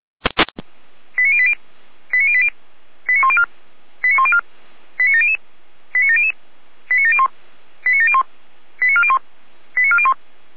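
Five-tone selective call (ZVEI), heard as quick runs of five tones, tones of about 70 ms. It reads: 89080 89080 89104 89104 89700 89700 89701 89701 89401 89401
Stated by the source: Five-tone ZVEI selective call received on an SDR.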